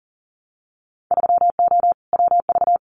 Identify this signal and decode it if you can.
Morse code, heard as a keyed tone, about 41 wpm, 700 Hz. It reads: 3O WV
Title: Winter Field Day Exchanges 40wpm